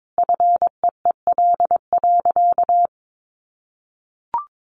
Morse code, heard as a keyed tone, about 22 wpm, 700 Hz. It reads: FEEL.
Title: Sentences from Top 1000 Words 22wpm